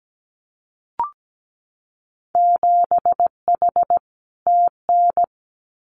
Morse code, heard as a keyed tone, about 17 wpm, 700 Hz. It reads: 7H TN
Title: Winter Field Day Exchanges Encore 17wpm